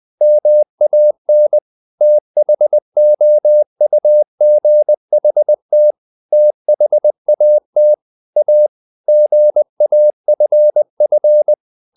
Morse code, heard as a keyed tone, about 20 wpm, 600 Hz. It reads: MAN THOUGHT THAT A GAFF
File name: the old man and the sea / the old man and the sea part 1 20wpm